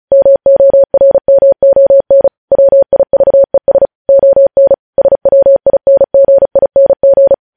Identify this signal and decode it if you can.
Morse code, heard as a keyed tone, about 36 wpm, 550 Hz. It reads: MORMON WIVES ON SWINGING
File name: podcast / News headlines-1x-35x35